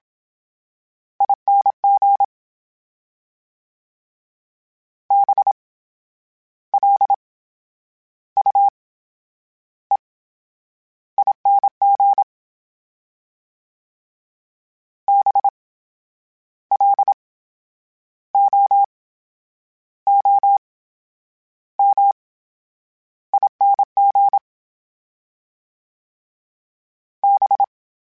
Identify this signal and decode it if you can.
Morse code, heard as a keyed tone, about 27 wpm, spaced as for 9 wpm, 800 Hz. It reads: ING BLUEING BLOOMING B
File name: Special INGs / Special INGs-mp3s / VerySlowToFaster Ing @25